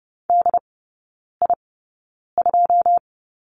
Morse code, heard as a keyed tone, about 30 wpm, spaced as for 9 wpm, 700 Hz. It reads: DI2
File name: 081 Lesson 14e - Number 2 - Three Characters 30wpm